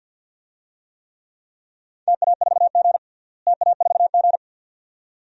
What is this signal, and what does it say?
Morse code, heard as a keyed tone, about 50 wpm, 700 Hz. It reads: TA4G TA4G
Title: DXCC-100 Call Signs - 4 Characters - Repeated 2x - 50wpm